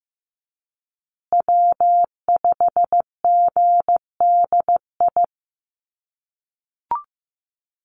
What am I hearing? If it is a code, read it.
Morse code, heard as a keyed tone, about 15 wpm, 700 Hz. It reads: W5GDI
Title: USA Call Signs Repeated 2x 15wpm